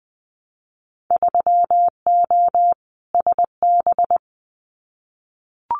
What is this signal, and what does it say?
Morse code, heard as a keyed tone, about 20 wpm, 700 Hz. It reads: 3O SB
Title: Winter Field Day Exchanges Encore 20wpm